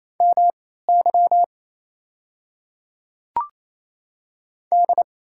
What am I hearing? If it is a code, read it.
Morse code, heard as a keyed tone, about 28 wpm, spaced as for 16 wpm, 690 Hz. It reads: MY D
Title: 3x Character Spacing - Top 200 Words - 28wpm